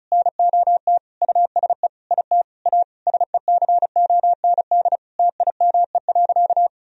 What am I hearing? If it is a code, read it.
Morse code, heard as a keyed tone, about 34 wpm, 700 Hz. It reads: NOT USE IT A SECOND TIME.
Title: Sentences from Top 1000 Words - Review Part 1 - 35wpm